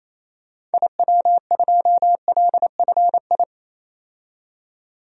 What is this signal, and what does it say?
Morse code, heard as a keyed tone, about 28 wpm, 700 Hz. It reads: IW2LFI
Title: DXCC-100 Call Signs - 6 Characters - 28wpm